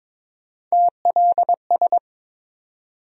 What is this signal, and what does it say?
Morse code, heard as a keyed tone, about 22 wpm, 700 Hz. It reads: TLS
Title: USA Call Sign Postfixes - 3 Letters 22wpm